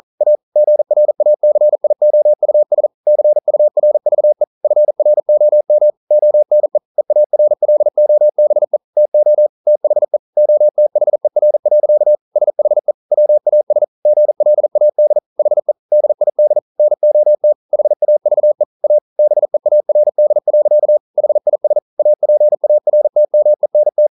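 Morse code, heard as a keyed tone, about 40 wpm, 600 Hz. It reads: A GRACIOUS CURVE FROM ONE EARLOBE TO THE OTHER. SHE WAS GLAD HE DID NOT HAVE A BEARD. HIS APARTMENT
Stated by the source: Morse code practice at 41 wpm, from CW med Gnister.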